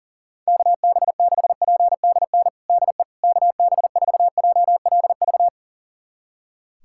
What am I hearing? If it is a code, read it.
Morse code, heard as a keyed tone, about 40 wpm, 700 Hz. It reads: KB6PDN DE KB4JLV
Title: USA Callsign DE Callsign - 6 Characters - 40wpm